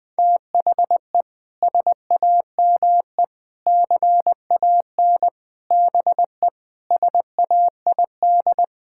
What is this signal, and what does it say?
Morse code, heard as a keyed tone, about 21 wpm, 700 Hz. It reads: THE SAME CAN BE SAID
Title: Sentences with 6-Word Phrases - 20wpm